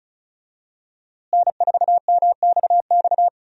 Morse code, heard as a keyed tone, about 34 wpm, 700 Hz. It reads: N4MXX
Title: USA Call Signs - 5 Characters - Repeated 3x - 35wpm